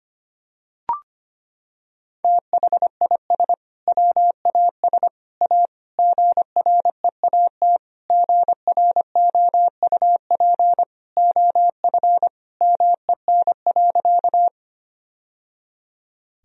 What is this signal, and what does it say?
Morse code, heard as a keyed tone, about 25 wpm, 700 Hz. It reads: THIS WAS A GREAT GROUP OF MEN.
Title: Sentences from Top 1000 Words - Review Part 1 - 25wpm